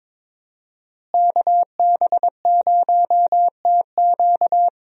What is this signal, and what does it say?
Morse code, heard as a keyed tone, about 22 wpm, 700 Hz. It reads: KB0TQ
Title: USA Call Signs - 5 Characters - Repeated 3x - 22wpm